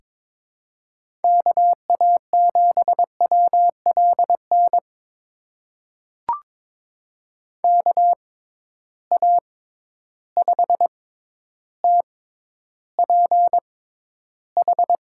Morse code, heard as a keyed tone, about 22 wpm, spaced as for 11 wpm, 700 Hz. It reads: KA7WLN KA5TPH
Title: USA Call Signs - 6 Characters - 6x Character Spacing - 22wpm